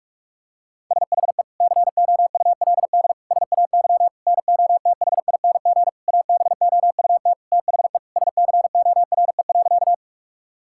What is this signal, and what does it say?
Morse code, heard as a keyed tone, about 41 wpm, 700 Hz. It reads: SHE COULD SAY NOTHING ABOUT THE SCORE.
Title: Sentences from Top 1600 Words 45wpm